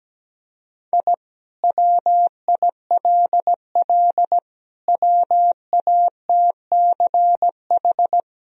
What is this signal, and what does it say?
Morse code, heard as a keyed tone, about 17 wpm, 700 Hz. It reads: I WILL WATCH
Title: Sentences from Top 1000 Words - Review Part 1 - 17wpm